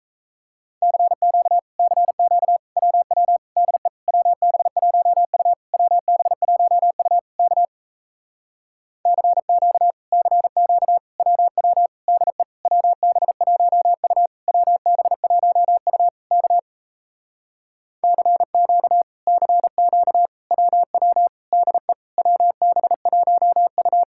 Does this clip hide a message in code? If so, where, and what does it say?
Yes: Morse code from 0.8 to 24.0 s, heard as a keyed tone, about 38 wpm, 700 Hz. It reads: CQ CQ WW DE WB1U WB1U K CQ CQ WW DE WB1U WB1U K CQ CQ WW DE WB1U